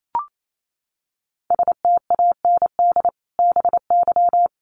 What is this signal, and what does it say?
Morse code, heard as a keyed tone, about 28 wpm, 700 Hz. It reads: STAND BY
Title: Sentences from Top 1200 Words 28wpm